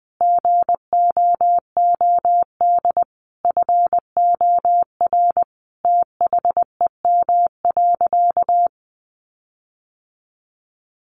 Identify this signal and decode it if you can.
Morse code, heard as a keyed tone, about 19 wpm, 700 Hz. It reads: GOOD FOR THEM.
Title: Sentences from Top 1300 Words 20wpm